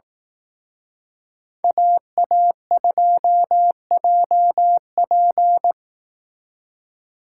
Morse code, heard as a keyed tone, about 18 wpm, 700 Hz. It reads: AA2JP
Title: Speed Racing - USA 2x2 Call Signs 15wpm